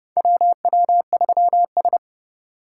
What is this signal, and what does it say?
Morse code, heard as a keyed tone, about 29 wpm, 700 Hz. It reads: WW3S
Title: USA Call Signs - 4 Characters - 4x Character Spacing - 30wpm